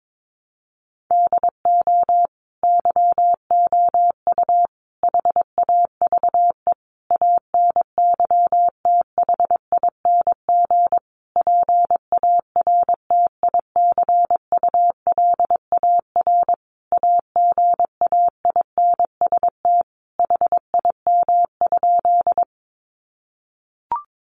Morse code, heard as a keyed tone, about 22 wpm, 700 Hz. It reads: DO YOU HAVE ANYTHING PARTICULAR AGAINST HIM?
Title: Sentences from Top 1000 Words 22wpm